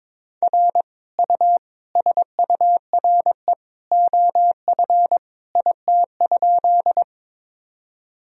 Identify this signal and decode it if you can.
Morse code, heard as a keyed tone, about 22 wpm, 700 Hz. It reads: R U SURE OF IT?
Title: Sentences from Top 300 Words and Abbreviations 22wpm